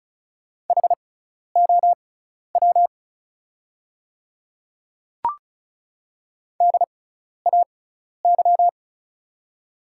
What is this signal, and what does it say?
Morse code, heard as a keyed tone, about 36 wpm, spaced as for 12 wpm, 700 Hz. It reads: HOW DAY
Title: 6x Character Spacing - Top 100 Words - 35wpm